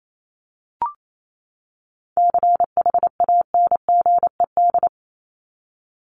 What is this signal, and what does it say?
Morse code, heard as a keyed tone, about 28 wpm, 700 Hz. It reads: CHANGED
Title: ED Suffix - Top 100 Words with Base Words 28wpm